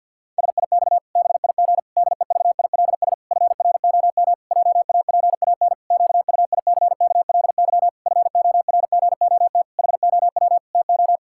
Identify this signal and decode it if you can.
Morse code, heard as a keyed tone, about 51 wpm, 700 Hz. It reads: SIX BIG DEVILS FROM JAPAN QUICKLY FORGOT HOW TO
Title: Pangram Challenge v2 - 50wpm